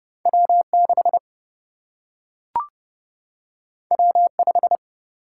Morse code, heard as a keyed tone, about 30 wpm, 700 Hz. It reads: W 6 W 5
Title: Single-Letter Single-Number 30wpm